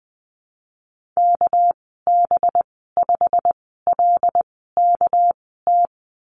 Morse code, heard as a keyed tone, about 20 wpm, spaced as for 15 wpm, 700 Hz. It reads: KB5LKT
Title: 213 Lesson 34g - Number 8 - Call Signs 20wpm